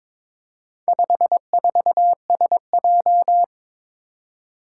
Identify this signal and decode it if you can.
Morse code, heard as a keyed tone, about 22 wpm, 700 Hz. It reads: 54SJ
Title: Four Challenging Letter-Number Combinations 22wpm